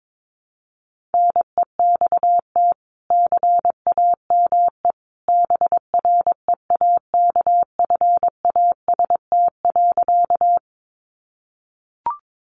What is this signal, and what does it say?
Morse code, heard as a keyed tone, about 22 wpm, 700 Hz. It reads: NEXT CAME BREAKFAST.
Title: Sentences from Top 1500 Words 22wpm